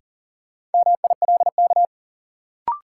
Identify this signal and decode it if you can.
Morse code, heard as a keyed tone, about 40 wpm, 700 Hz. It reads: MILK
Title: Sentences from Top 1500 Words - New Word Drill - 40wpm